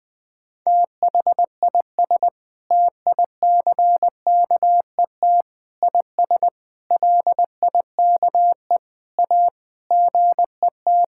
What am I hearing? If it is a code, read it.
Morse code, heard as a keyed tone, about 20 wpm, 700 Hz. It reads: THIS TICKET IS LIKE A GET
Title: Sentences with 7-Word Phrases - Spoken Beginning - 20wpm